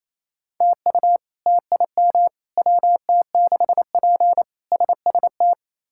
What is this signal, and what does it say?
Morse code, heard as a keyed tone, about 27 wpm, 700 Hz. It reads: TU TIM WT6P SST
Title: K1USN SST QSO Practice - 28wpm